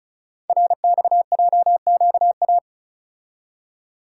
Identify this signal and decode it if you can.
Morse code, heard as a keyed tone, about 36 wpm, 700 Hz. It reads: RXJQA
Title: Code Groups - 5 Letters - 35wpm